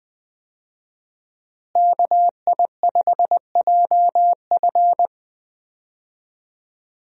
Morse code, heard as a keyed tone, about 20 wpm, 700 Hz. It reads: KI5JF
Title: USA Call Signs - 5 Characters - Repeated 3x - 20wpm